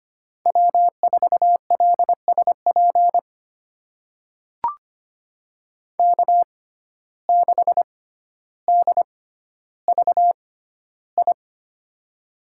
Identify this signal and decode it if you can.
Morse code, heard as a keyed tone, about 25 wpm, spaced as for 13 wpm, 700 Hz. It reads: W4LSP K6DVI
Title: USA Call Signs - 5 Characters - 6x Character Spacing - 25wpm